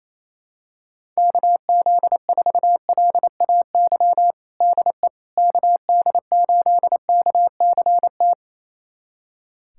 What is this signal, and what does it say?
Morse code, heard as a keyed tone, about 28 wpm, 700 Hz. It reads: KZ4LAY DE KD8KCT
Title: USA Callsign DE Callsign - 6 Characters - 28wpm